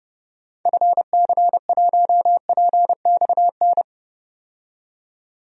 Morse code, heard as a keyed tone, about 32 wpm, 700 Hz. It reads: FC1PXN
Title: DXCC-100 Call Signs - 6 Characters - 30wpm